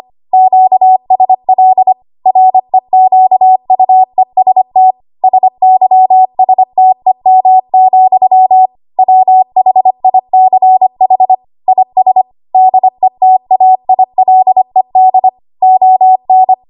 Morse code, heard as a keyed tone, about 25 wpm, 750 Hz. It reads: QSL REQUEST SYSTEM, WHICH IS DETAILED ON
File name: Archive / 25 WPM / 25WPM